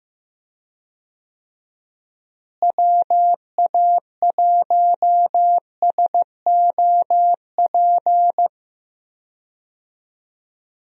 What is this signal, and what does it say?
Morse code, heard as a keyed tone, about 14 wpm, 700 Hz. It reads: WA1SOP
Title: USA Call Signs - 6 Characters - Repeated 2x - 15wpm